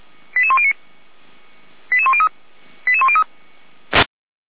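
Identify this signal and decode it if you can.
Five-tone selective call (ZVEI), heard as quick runs of five tones, tones of about 80 ms. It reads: 88109 88103 88103
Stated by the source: ZVEI five-tone selective call received on SDR, fire brigade feuerwehr-orlamuende paging, code 88102.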